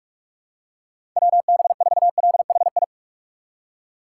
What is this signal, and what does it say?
Morse code, heard as a keyed tone, about 45 wpm, 700 Hz. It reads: WB4LHI
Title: USA Call Signs - 6 Characters - 4x Character Spacing - 45wpm